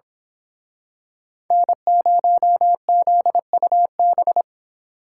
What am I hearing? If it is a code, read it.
Morse code, heard as a keyed tone, about 26 wpm, 700 Hz. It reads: N0ZUB
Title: Speed Racing - USA 1x3 Call Signs 22wpm